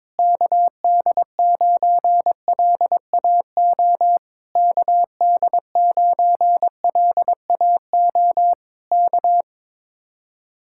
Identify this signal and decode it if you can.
Morse code, heard as a keyed tone, about 22 wpm, 700 Hz. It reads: KD9LAO KD9LAO K
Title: Speed-Racing Common QSO Phrases - 22wpm